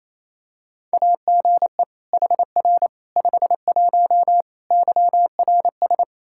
Morse code, heard as a keyed tone, about 27 wpm, 700 Hz. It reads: AGE HR 51 YRS
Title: Practice QSO - Exchange 3 - More Info - 28wpm